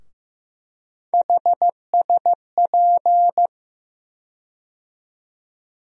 Morse code, heard as a keyed tone, about 15 wpm, 700 Hz. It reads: HSP
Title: USA Call Sign Postfixes - 3 Letters 15wpm